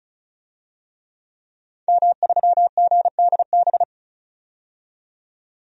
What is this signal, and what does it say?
Morse code, heard as a keyed tone, about 36 wpm, 700 Hz. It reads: M3GDB
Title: DXCC-100 Call Signs - 5 Characters - 35wpm